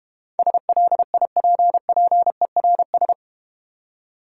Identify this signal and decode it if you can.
Morse code, heard as a keyed tone, about 32 wpm, 700 Hz. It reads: SLIPPERS